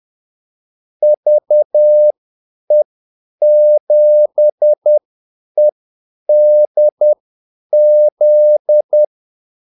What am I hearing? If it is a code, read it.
Morse code, heard as a keyed tone, about 10 wpm, spaced as for 8 wpm, 600 Hz. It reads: VE7EDZ
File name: Callsigns-10-8wpm